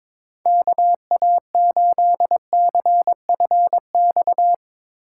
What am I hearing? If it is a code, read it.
Morse code, heard as a keyed tone, about 22 wpm, 700 Hz. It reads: KA8CFX
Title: Speed Racing - USA Call Signs 22wpm